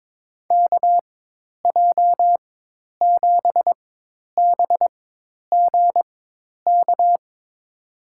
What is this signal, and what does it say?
Morse code, heard as a keyed tone, about 22 wpm, spaced as for 10 wpm, 700 Hz. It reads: KJ7BGK
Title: USA Call Signs - 4x Character Spacing - 22wpm